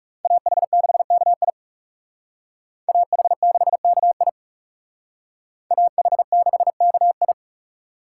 Morse code, heard as a keyed tone, about 38 wpm, 700 Hz. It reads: AH6KI AH6KI AH6KI